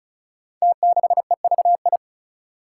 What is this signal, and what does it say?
Morse code, heard as a keyed tone, about 36 wpm, 700 Hz. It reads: T6EVI
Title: Five Challenging Letter-Number Combinations 35wpm